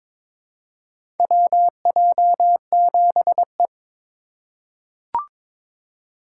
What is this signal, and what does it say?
Morse code, heard as a keyed tone, about 22 wpm, 700 Hz. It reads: WJ7E
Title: USA Call Signs - 4 Characters - 2x Character Spacing - 22wpm